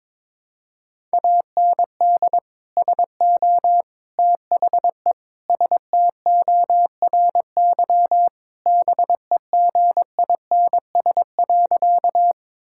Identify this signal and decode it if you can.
Morse code, heard as a keyed tone, about 22 wpm, 700 Hz. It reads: AND SO THE STORY BEGINS.
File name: practice / 22wpm / Sentences with 2-Word Phrases - 22wpm